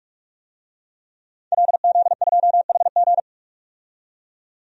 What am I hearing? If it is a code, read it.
Morse code, heard as a keyed tone, about 46 wpm, 700 Hz. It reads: LZ2HG
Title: DXCC-100 Call Signs - 5 Characters - 45wpm